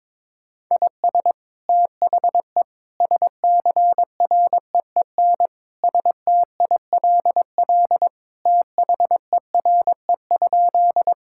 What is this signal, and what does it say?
Morse code, heard as a keyed tone, about 21 wpm, 700 Hz. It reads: IS THE SCREEN STILL THERE?
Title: Sentences from Top 1000 Words - Review Part 6 - 22wpm